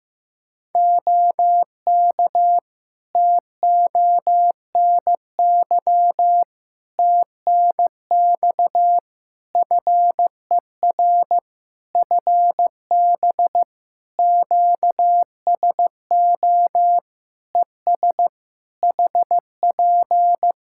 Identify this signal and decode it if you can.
Morse code, heard as a keyed tone, about 15 wpm, 700 Hz. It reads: OK TONY TNX FER FB QSO ES HP
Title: Practice QSO - Exchange 4 - Ending - 15wpm